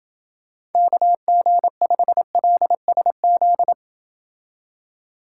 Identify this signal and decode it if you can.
Morse code, heard as a keyed tone, about 30 wpm, 700 Hz. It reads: KG5LSZ